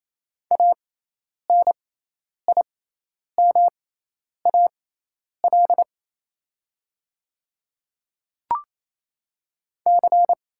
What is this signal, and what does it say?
Morse code, heard as a keyed tone, about 28 wpm, spaced as for 10 wpm, 700 Hz. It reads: ANIMAL C
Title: 6x Character Spacing - Top 200 Words - 28wpm